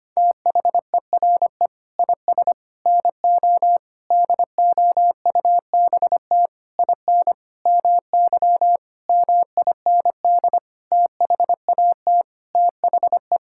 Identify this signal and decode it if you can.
Morse code, heard as a keyed tone, about 25 wpm, 700 Hz. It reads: THERE IS NO DOUBT IN MY MIND THAT THE
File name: Sets of 9 Words - 25wpm